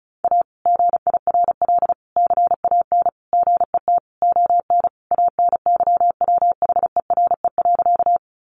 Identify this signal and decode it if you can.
Morse code, heard as a keyed tone, about 34 wpm, 700 Hz. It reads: A GIRL CAN GET ON ANYWHERE.
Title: Sentences from Top 1400 Words 35wpm